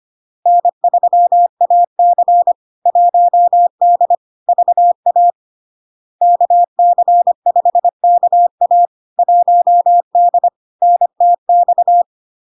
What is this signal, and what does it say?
Morse code, heard as a keyed tone, about 25 wpm, 700 Hz. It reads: N3AC 1D VA KC5KA 1D NTX
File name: LARK-FD-2022-Log-in-CW-25wpm-700-Hz